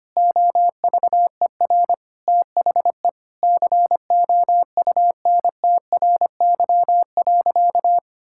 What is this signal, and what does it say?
Morse code, heard as a keyed tone, about 25 wpm, 700 Hz. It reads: OVER THE COUNTRY.
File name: practice / 25wpm / Sentences with 4-Word Phrases - 25wpm